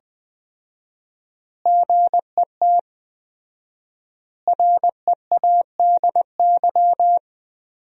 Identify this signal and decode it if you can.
Morse code, heard as a keyed tone, about 21 wpm, 700 Hz. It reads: GET READY
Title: 4x Word-Spacing - Sentences from Top 400 Words - 20wpm